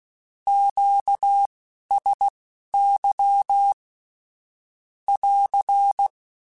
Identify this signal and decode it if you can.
Morse code, heard as a keyed tone, about 16 wpm, spaced as for 11 wpm, 800 Hz. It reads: QSY <AR>
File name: (spaced) / Q-Calls Spaced-mp3s / Q Calls sp Once @15